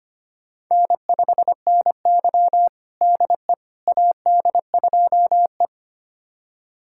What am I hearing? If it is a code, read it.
Morse code, heard as a keyed tone, about 25 wpm, 700 Hz. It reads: N5NY DE AD2E
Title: USA Callsign DE Callsign - 4 Characters - 25wpm